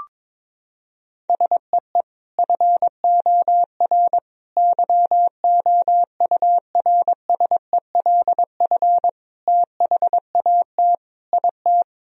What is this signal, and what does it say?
Morse code, heard as a keyed tone, about 22 wpm, 700 Hz. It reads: SEE FOR YOURSELF THAT IT
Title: Sentences from Top 1000 Words - Review Part 1 - 22wpm